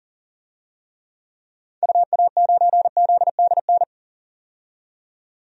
Morse code, heard as a keyed tone, about 40 wpm, 700 Hz. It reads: UA9ZDN
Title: DXCC-100 Call Signs - 6 Characters - 40wpm